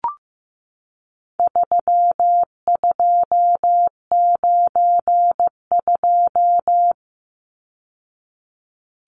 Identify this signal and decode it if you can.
Morse code, heard as a keyed tone, about 15 wpm, 700 Hz. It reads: 3292
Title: Numbers - 4-Digit 15wpm